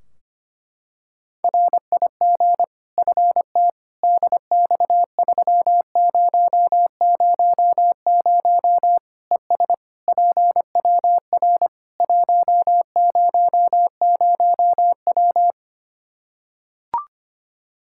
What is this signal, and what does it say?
Morse code, heard as a keyed tone, about 25 wpm, 700 Hz. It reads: RIG FT DX3000 ES PWR 100W
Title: Practice QSO - Exchange 2 - Info - 25wpm